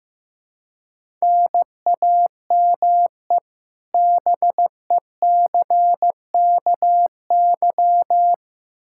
Morse code, heard as a keyed tone, about 16 wpm, 700 Hz. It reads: NAME BECKY
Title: Practice QSO - Exchange 1 - Beginning - 15wpm